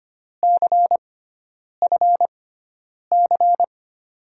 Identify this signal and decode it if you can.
Morse code, heard as a keyed tone, about 25 wpm, spaced as for 9 wpm, 700 Hz. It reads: CFC
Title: USA Call Signs - 6x Character Spacing - 25wpm